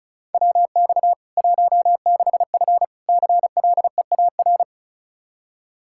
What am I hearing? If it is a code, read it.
Morse code, heard as a keyed tone, about 36 wpm, 700 Hz. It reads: WX 16F CLEAR
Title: K1USN SST QSO Practice 35wpm